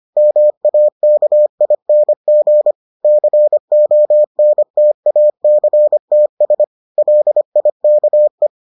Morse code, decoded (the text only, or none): MAKING CONTACTS LIKE